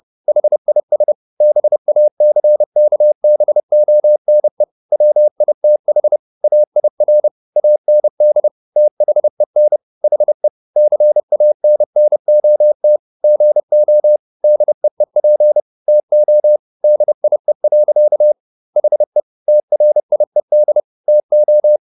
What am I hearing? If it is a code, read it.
Morse code, heard as a keyed tone, about 30 wpm, 600 Hz. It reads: HIS BACKBONE WITH AIR AND THEN HE CANNOT GO DEEP TO DIE. HE TRIED TO
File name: the old man and the sea / the old man and the sea 1 30wpm